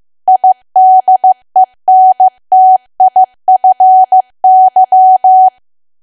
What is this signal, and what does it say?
Morse code, heard as a keyed tone, about 15 wpm, 750 Hz. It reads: IDENTIFY
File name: cwa / W1AW-Level2 / 15WPM